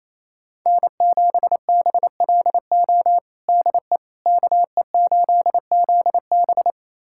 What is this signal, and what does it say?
Morse code, heard as a keyed tone, about 28 wpm, 700 Hz. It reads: N7BLO DE KE8ZB
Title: USA Callsign DE Callsign - 5 Characters - 28wpm